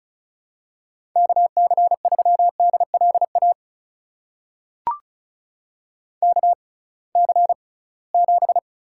Morse code, heard as a keyed tone, about 36 wpm, spaced as for 23 wpm, 700 Hz. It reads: KC3DLA KC7
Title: USA Call Signs - 6 Characters - 6x Character Spacing - 35wpm